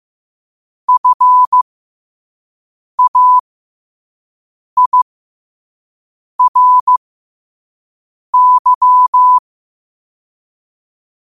Morse code, heard as a keyed tone, about 15 wpm, spaced as for 5 wpm, 1000 Hz. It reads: FAIRY
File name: 5wpm